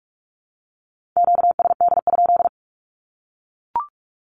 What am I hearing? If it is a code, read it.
Morse code, heard as a keyed tone, about 44 wpm, 700 Hz. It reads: QSD?
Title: Q-Codes Repeated 3x 45wpm